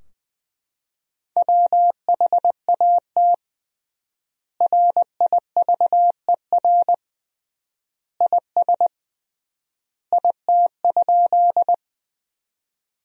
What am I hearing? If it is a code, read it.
Morse code, heard as a keyed tone, about 21 wpm, 700 Hz. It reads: WHAT RIVER IS IT?
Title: 3x Word-Spacing - Sentences from Top 300 Words - 20wpm